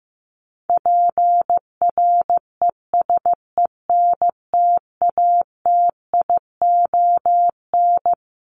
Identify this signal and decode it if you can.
Morse code, heard as a keyed tone, about 15 wpm, 700 Hz. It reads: PRESENTATION
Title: Top 1501-1750 Words 15wpm